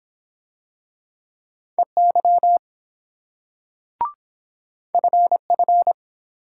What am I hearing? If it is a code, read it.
Morse code, heard as a keyed tone, about 22 wpm, 700 Hz. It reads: EY FF